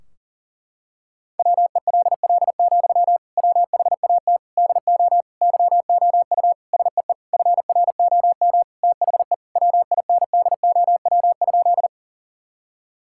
Morse code, heard as a keyed tone, about 40 wpm, 700 Hz. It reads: WELL, WHAT DO YOU SEE FROM THE WINDOW?
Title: Sentences from Top 1600 Words 40wpm